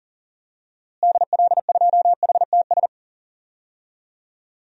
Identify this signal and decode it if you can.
Morse code, heard as a keyed tone, about 40 wpm, 700 Hz. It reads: DL2HTS
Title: DXCC-100 Call Signs - 6 Characters - Repeated 3x - 40wpm